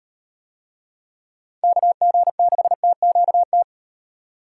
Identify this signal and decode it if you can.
Morse code, heard as a keyed tone, about 35 wpm, 700 Hz. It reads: KG6TQT